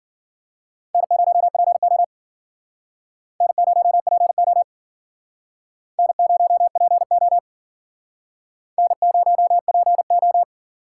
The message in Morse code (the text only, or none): N0PO N0PO N0PO N0PO